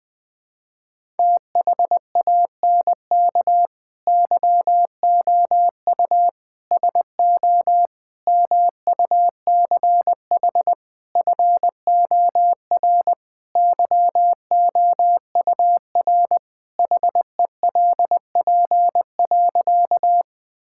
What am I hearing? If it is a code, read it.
Morse code, heard as a keyed tone, about 20 wpm, 700 Hz. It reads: THANK YOU SO MUCH FOR YOUR HELP.
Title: Sentences with 5-Word Phrases - 20wpm